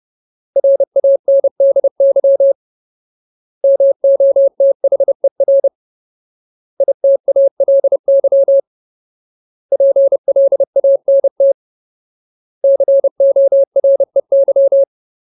Morse code, heard as a keyed tone, about 30 wpm, 550 Hz. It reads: RANDY MOTHER ITALY PLANT COREY
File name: Ultimate Word List / 30WPM Ultimate